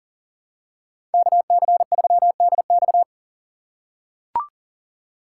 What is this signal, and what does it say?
Morse code, heard as a keyed tone, about 40 wpm, 690 Hz. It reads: KC3DX
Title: USA Call Signs - 5 Characters - 3x Character Spacing - 40wpm